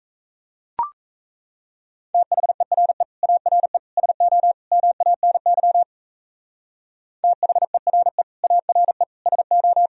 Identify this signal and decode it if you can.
Morse code, heard as a keyed tone, about 36 wpm, 700 Hz. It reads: THERE ARE SO MANY THERE ARE SO